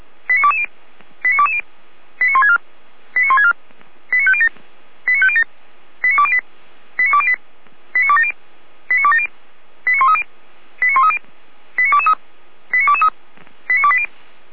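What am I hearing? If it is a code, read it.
Five-tone selective call (ZVEI), heard as quick runs of five tones, tones of about 70 ms. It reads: 89220 89220 89164 89164 89507 89507 89208 89208 89270 89270 89130 89130 89223 89223 89280
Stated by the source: ZVEI five-tone selective call, code 89150, received by SDR.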